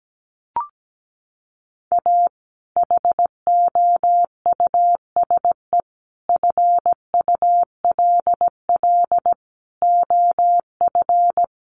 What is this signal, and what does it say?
Morse code, heard as a keyed tone, about 17 wpm, 700 Hz. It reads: A HOUSE FULL OF
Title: Sentences from Top 1500 Words 17wpm